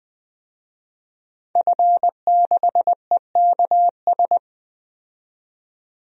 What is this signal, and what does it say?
Morse code, heard as a keyed tone, about 20 wpm, 700 Hz. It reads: F6EKS
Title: DXCC-100 Call Signs - 5 Characters - 20wpm